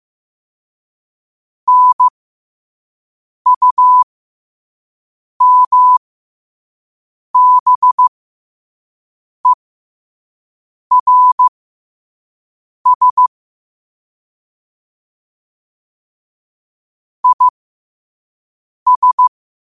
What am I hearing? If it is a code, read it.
Morse code, heard as a keyed tone, about 15 wpm, spaced as for 5 wpm, 1000 Hz. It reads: NUMBERS IS
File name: qotdfiles / 5wpm